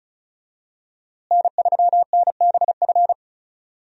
Morse code, heard as a keyed tone, about 36 wpm, 700 Hz. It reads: N3NBF
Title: USA Call Signs - 5 Characters - Repeated 2x - 35wpm